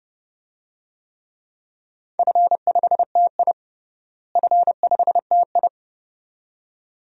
Morse code, heard as a keyed tone, about 30 wpm, 700 Hz. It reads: F5TI F5TI
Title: DXCC-100 Call Signs - 4 Characters - Repeated 2x - 30wpm